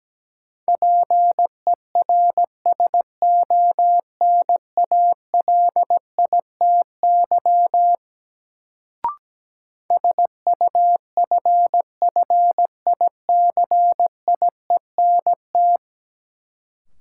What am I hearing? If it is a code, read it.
Morse code, heard as a keyed tone, about 17 wpm, 700 Hz. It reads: PERSONALITY SUFFICIENT
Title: Sentences from Top 1400 Words - New Word Drill Alt - 17wpm